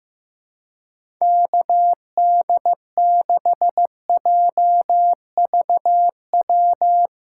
Morse code, heard as a keyed tone, about 16 wpm, 700 Hz. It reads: KD6JVW
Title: USA Call Signs - 4x Character Spacing - 15wpm